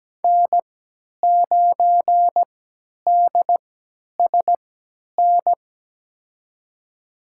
Morse code, heard as a keyed tone, about 17 wpm, spaced as for 10 wpm, 700 Hz. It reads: N9DSN
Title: USA Call Signs - 3x Character Spacing - 17wpm